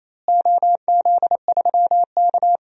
Morse code, heard as a keyed tone, about 28 wpm, 700 Hz. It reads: OZ3K
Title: DXCC-100 Call Signs - 4 Characters - Repeated 3x - 28wpm